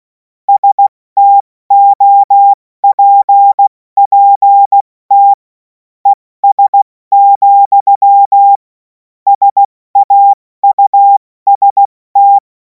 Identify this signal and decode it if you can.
Morse code, heard as a keyed tone, about 16 wpm, spaced as for 14 wpm, 800 Hz. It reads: STOPPT ES, SAUST
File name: Morsekurs / Witze / Witz 28. April 2026 70 F80